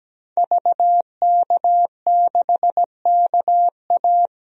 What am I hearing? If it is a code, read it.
Morse code, heard as a keyed tone, about 17 wpm, 700 Hz. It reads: VK6KA
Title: DXCC-100 Call Signs - 5 Characters - Repeated 2x - 17wpm